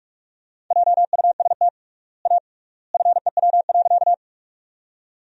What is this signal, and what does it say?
Morse code, heard as a keyed tone, about 46 wpm, 700 Hz. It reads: JUST A FEW.
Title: 3x Word-Spacing - Sentences from Top 300 Words - 45wpm